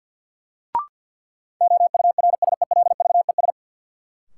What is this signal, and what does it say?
Morse code, heard as a keyed tone, about 51 wpm, 700 Hz. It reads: OURSELVES
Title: Sentences from Top 1600 Words - New Word Drill - 50wpm